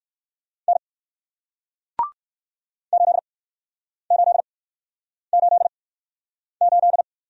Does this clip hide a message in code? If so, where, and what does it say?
Yes: Morse code from 0.7 to 7.0 s, heard as a keyed tone, about 61 wpm, 700 Hz. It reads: N 8 8 8 8